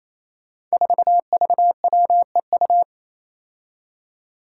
Morse code, heard as a keyed tone, about 27 wpm, 700 Hz. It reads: 4VWEU
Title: Five Challenging Letter-Number Combinations 28wpm